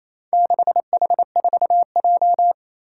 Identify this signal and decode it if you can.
Morse code, heard as a keyed tone, about 27 wpm, 700 Hz. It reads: 6H4J
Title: Four Challenging Letter-Number Combinations 28wpm